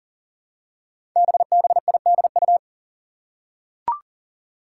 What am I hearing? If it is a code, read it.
Morse code, heard as a keyed tone, about 38 wpm, 700 Hz. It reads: BBIDU
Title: Five Challenging Letter-Number Combinations 40wpm